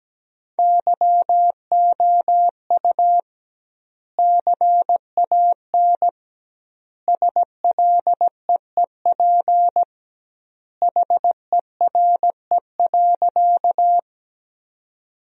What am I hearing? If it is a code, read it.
Morse code, heard as a keyed tone, about 17 wpm, 700 Hz. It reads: YOU CAN SLEEP HERE.
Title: 2x Word-Spacing - Sentences from Top 500 Words - 17wpm